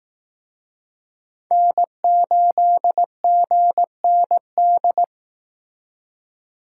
Morse code, heard as a keyed tone, about 18 wpm, 700 Hz. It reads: N8GND
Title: Speed Racing - USA 1x3 Call Signs 15wpm